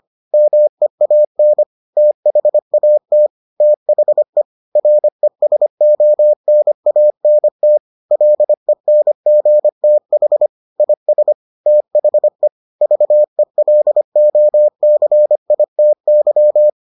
Morse code, heard as a keyed tone, about 25 wpm, 600 Hz. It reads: MEAN THAT THE RESONANT LENGTH IS THE VELOCITY